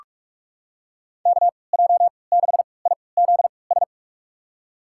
Morse code, heard as a keyed tone, about 46 wpm, spaced as for 26 wpm, 700 Hz. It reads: KJ6IZS
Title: USA Call Signs - 3x Character Spacing - 45wpm